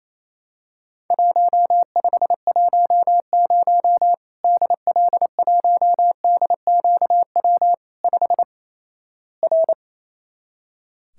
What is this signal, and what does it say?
Morse code, heard as a keyed tone, about 28 wpm, 700 Hz. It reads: 1510 DL1DQW 5 R
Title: Worked All Europe DX Contest - QTC Exchange - 28wpm